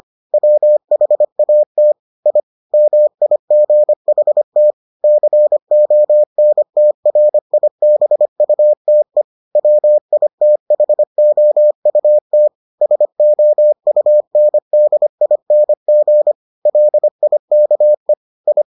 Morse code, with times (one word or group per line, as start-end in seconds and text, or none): 0.3-1.9 WHAT
2.3-2.4 I
2.7-4.7 MIGHT
5.0-9.2 CONTRIBUTE
9.6-12.5 WITHOUT
12.8-16.3 SOUNDING
16.7-18.1 LIKE
18.5-18.6 I